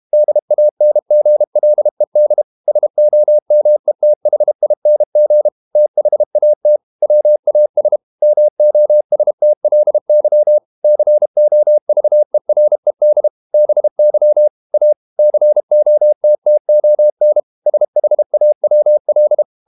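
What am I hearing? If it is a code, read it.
Morse code, heard as a keyed tone, about 32 wpm, 600 Hz. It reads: DANGLED SOMETHING THAT WAS MOSTLY COVERED BY A COTTON SHAWL
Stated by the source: Morse code practice at 32 wpm from CW med Gnister, reading Ken Follett's novel Never in English.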